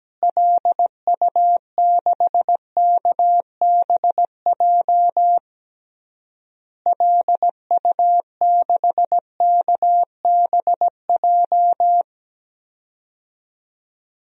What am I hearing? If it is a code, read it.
Morse code, heard as a keyed tone, about 17 wpm, 700 Hz. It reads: LU6KBJ LU6KBJ
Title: DXCC-100 Call Signs - 6 Characters - Repeated 3x - 17wpm